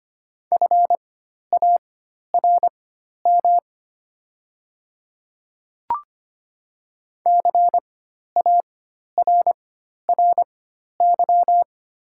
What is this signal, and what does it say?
Morse code, heard as a keyed tone, about 25 wpm, spaced as for 12 wpm, 700 Hz. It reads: FARM CARRY
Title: 4x Character Spacing - Top 500 Words - 25wpm